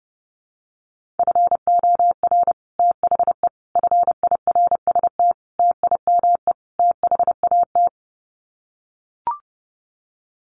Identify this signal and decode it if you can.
Morse code, heard as a keyed tone, about 30 wpm, 700 Hz. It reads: FOR THE FIRST TIME THAT
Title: Sets of 5 Words Encore 30wpm